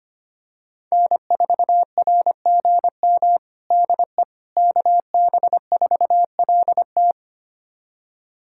Morse code, heard as a keyed tone, about 25 wpm, 700 Hz. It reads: N4RGM DE KB4LT
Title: USA Callsign DE Callsign - 5 Characters - 25wpm